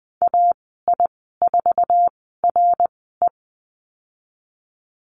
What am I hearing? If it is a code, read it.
Morse code, heard as a keyed tone, about 19 wpm, spaced as for 14 wpm, 700 Hz. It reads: AI4RE
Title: 055 Lesson 10g - Letter r - Call Signs 20wpm